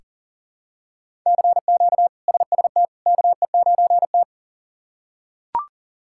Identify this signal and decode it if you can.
Morse code, heard as a keyed tone, about 40 wpm, 700 Hz. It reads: CQ SST KE9T
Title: K1USN SST QSO Practice - 40wpm